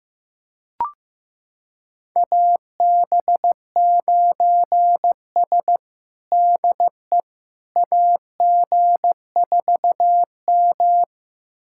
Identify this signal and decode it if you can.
Morse code, heard as a keyed tone, about 15 wpm, 700 Hz. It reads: AB9S DE AG4M
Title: USA Callsign DE Callsign - 4 Characters - 15wpm